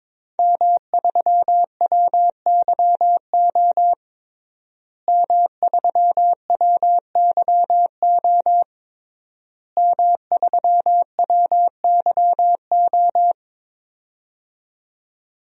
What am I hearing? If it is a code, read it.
Morse code, heard as a keyed tone, about 22 wpm, 700 Hz. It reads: M3WYO M3WYO M3WYO
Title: DXCC-100 Call Signs - 5 Characters - Repeated 3x - 22wpm